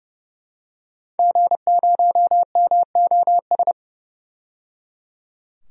Morse code, heard as a keyed tone, about 30 wpm, 700 Hz. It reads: G0MOS
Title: DXCC-100 Call Signs - 5 Characters - 30wpm